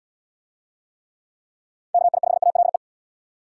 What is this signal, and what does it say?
Morse code, heard as a keyed tone, about 75 wpm, 700 Hz. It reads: KI5ALE